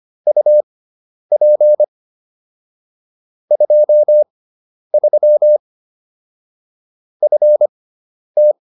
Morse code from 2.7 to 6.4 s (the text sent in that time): 23